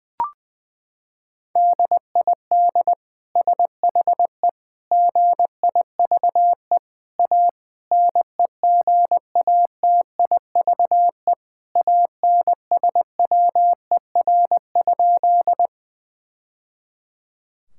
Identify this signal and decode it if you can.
Morse code, heard as a keyed tone, about 20 wpm, 700 Hz. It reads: DID SHE GIVE A NEGATIVE ANSWER?
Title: Sentences from Top 1000 Words - Review Part 6 - 20wpm